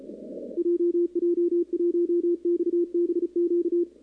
Morse code, heard as a keyed tone, about 34 wpm, 340 Hz, with radio noise. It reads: JJ1XBQ
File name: contests / 2019-cw-wpx-kt5j-k5pi- -2330 0-11